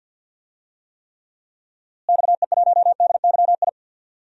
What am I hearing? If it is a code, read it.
Morse code, heard as a keyed tone, about 47 wpm, 700 Hz. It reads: XE1DYI
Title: DXCC-100 Call Signs - 6 Characters - Repeated 3x - 50wpm